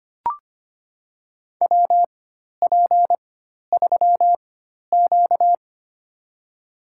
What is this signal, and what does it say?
Morse code, heard as a keyed tone, about 25 wpm, spaced as for 12 wpm, 700 Hz. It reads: WP3Q
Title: USA Call Signs - 4 Characters - 4x Character Spacing - 25wpm